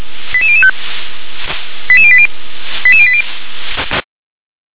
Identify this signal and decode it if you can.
Five-tone selective call (ZVEI), heard as quick runs of five tones, tones of about 70 ms. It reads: 88005 88080 88080